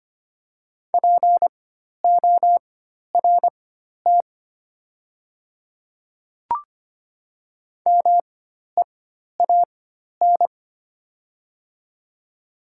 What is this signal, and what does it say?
Morse code, heard as a keyed tone, about 25 wpm, spaced as for 12 wpm, 700 Hz. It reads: PORT MEAN
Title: 4x Character Spacing - Top 200 Words - 25wpm